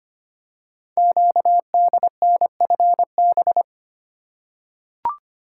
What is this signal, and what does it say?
Morse code, heard as a keyed tone, about 25 wpm, 700 Hz. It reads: QDNFB
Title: Code Groups - 5 Letters - 25wpm